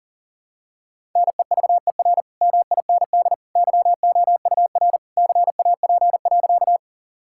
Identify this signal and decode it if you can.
Morse code, heard as a keyed tone, about 37 wpm, 700 Hz. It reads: NEVER MIND YOUR CAP.
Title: Sentences from Top 1600 Words 40wpm